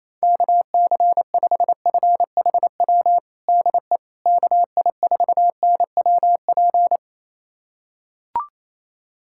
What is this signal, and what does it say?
Morse code, heard as a keyed tone, about 28 wpm, 700 Hz. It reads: KC5FHW DE KI4NWP
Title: USA Callsign DE Callsign 28wpm